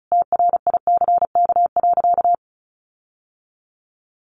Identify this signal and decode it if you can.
Morse code, heard as a keyed tone, about 34 wpm, 700 Hz. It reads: TRICK.
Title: Sentences from Top 1500 Words 35wpm